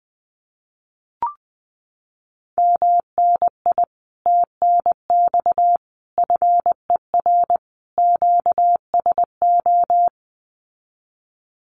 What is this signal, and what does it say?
Morse code, heard as a keyed tone, about 20 wpm, 700 Hz. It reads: MNI TNX FER QSO
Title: Common QSO Phrases Encore 20wpm